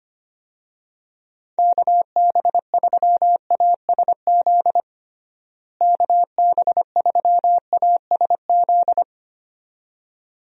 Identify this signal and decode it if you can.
Morse code, heard as a keyed tone, about 25 wpm, 700 Hz. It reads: KB3ASZ KB3ASZ
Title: USA Call Signs - 6 Characters - Repeated 2x - 25wpm